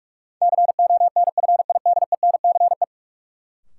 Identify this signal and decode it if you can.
Morse code, heard as a keyed tone, about 46 wpm, 700 Hz. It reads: CONFIDENCE
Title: Sentences from Top 1500 Words - New Word Drill Alt - 45wpm